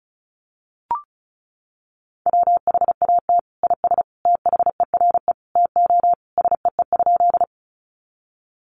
Morse code, heard as a keyed tone, about 34 wpm, 700 Hz. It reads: WHAT IS THERE TO SEE?
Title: Sentences from Top 100 Words 35wpm